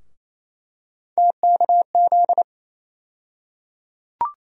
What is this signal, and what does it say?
Morse code, heard as a keyed tone, about 27 wpm, 700 Hz. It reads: TKZ
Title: USA Call Sign Postfixes - 3 Letters 28wpm